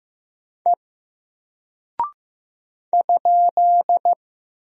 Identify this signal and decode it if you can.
Morse code, heard as a keyed tone, about 15 wpm, 700 Hz. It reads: E ?
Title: Single Character (Letter:Number:Punctuation:Prosign) Alt 15wpm